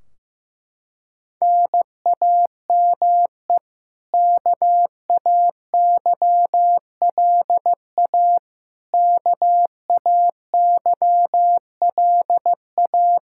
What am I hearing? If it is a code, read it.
Morse code, heard as a keyed tone, about 15 wpm, 700 Hz. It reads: NAME KAYLA KAYLA
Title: Practice QSO - Exchange 1 - Beginning - 15wpm